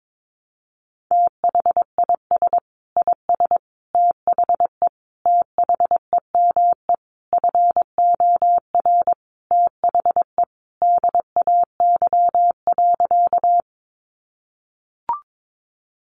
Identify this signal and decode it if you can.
Morse code, heard as a keyed tone, about 22 wpm, 700 Hz. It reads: THIS IS THE THEME FOR THE DAY.
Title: Sentences from Top 1100 Words 22wpm